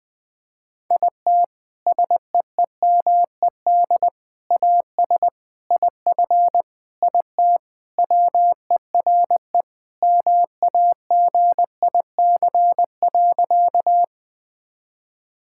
Morse code, heard as a keyed tone, about 20 wpm, 700 Hz. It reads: IT SEEMED AS IF IT WERE MAGIC.
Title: Sentences with 4-Word Phrases - 20wpm